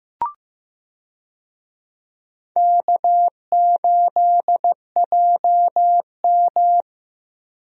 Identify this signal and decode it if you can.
Morse code, heard as a keyed tone, about 15 wpm, 700 Hz. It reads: K8JM
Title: USA Call Signs - 4 Characters - Repeated 3x - 15wpm